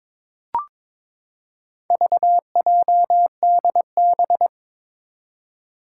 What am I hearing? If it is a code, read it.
Morse code, heard as a keyed tone, about 21 wpm, 700 Hz. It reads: VJDB
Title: Four Challenging Letter-Number Combinations 22wpm